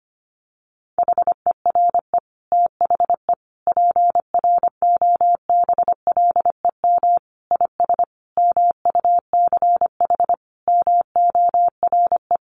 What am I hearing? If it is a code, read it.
Morse code, heard as a keyed tone, about 25 wpm, 700 Hz. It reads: HERE THE PROBLEM IS MUCH MORE
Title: Sentences from Top 1200 Words 25wpm